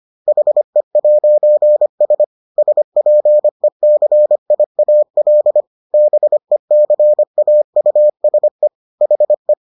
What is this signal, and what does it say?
Morse code, heard as a keyed tone, about 25 wpm, 600 Hz. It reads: HE'S SPECIAL BECAUSE HE